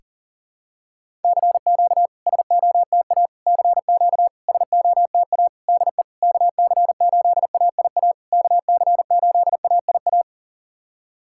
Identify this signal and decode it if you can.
Morse code, heard as a keyed tone, about 44 wpm, 700 Hz. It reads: CQ SOTA CQ SOTA DE KC8AIA KC8AIA
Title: SOTA QSO Practice 40wpm